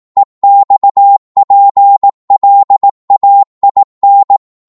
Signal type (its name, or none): Morse code